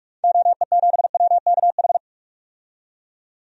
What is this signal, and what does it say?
Morse code, heard as a keyed tone, about 46 wpm, 700 Hz. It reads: OE7WKH
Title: DXCC-100 Call Signs - 6 Characters - 45wpm